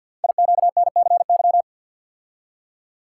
Morse code, heard as a keyed tone, about 53 wpm, 700 Hz. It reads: IQNCY
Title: Code Groups - 5 Letters - 50wpm